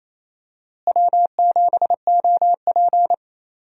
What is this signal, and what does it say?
Morse code, heard as a keyed tone, about 28 wpm, 700 Hz. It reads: W7OP
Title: USA Call Signs - 4 Characters - Repeated 3x - 28wpm